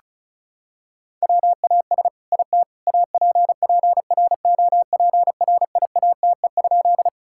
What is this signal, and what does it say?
Morse code, heard as a keyed tone, about 36 wpm, 700 Hz. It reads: WAS IT APPROPRIATE?
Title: Sentences from Top 1000 Words - Review Part 5 - 35wpm